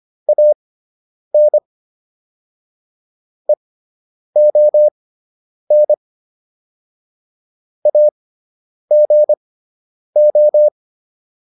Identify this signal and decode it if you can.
Morse code, heard as a keyed tone, about 25 wpm, spaced as for 9 wpm, 600 Hz. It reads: AN EON AGO